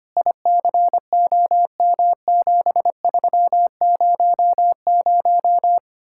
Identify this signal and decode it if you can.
Morse code, heard as a keyed tone, about 32 wpm, 700 Hz. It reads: ICOM7300